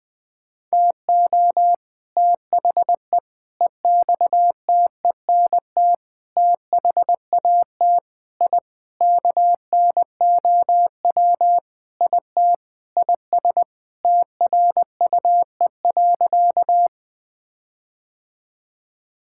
Morse code, heard as a keyed tone, about 20 wpm, 700 Hz. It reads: TO THE EXTENT THAT I KNOW IT IS TRUE.
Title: Sentences with 4-Word Phrases - Spoken Beginning - 20wpm